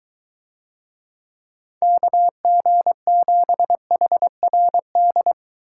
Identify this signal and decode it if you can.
Morse code, heard as a keyed tone, about 22 wpm, 700 Hz. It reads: KG7HRD